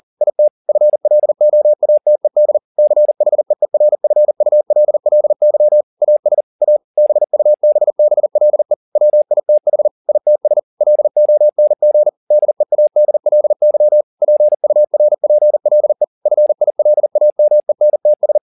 Morse code, heard as a keyed tone, about 40 wpm, 600 Hz. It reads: IT FLOATED CHEERFULLY AS A BUBBLE WITH ITS LONG DEADLY PURPLE FILAMENTS